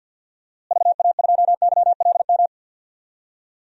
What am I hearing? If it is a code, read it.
Morse code, heard as a keyed tone, about 52 wpm, 700 Hz. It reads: VA2YLM
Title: DXCC-100 Call Signs - 6 Characters - Repeated 2x - 50wpm